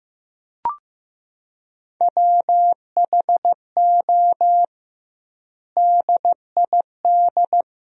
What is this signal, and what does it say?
Morse code, heard as a keyed tone, about 15 wpm, 700 Hz. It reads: WHO DID
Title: 2x Word-Spacing - Sentences from Top 300 Words - 15wpm